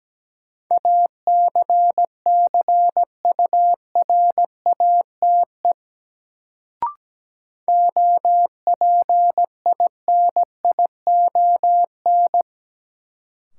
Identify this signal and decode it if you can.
Morse code, heard as a keyed tone, about 17 wpm, 700 Hz. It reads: ACCURATE OPINION
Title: Sentences from Top 1200 Words - New Word Drill Alt - 17wpm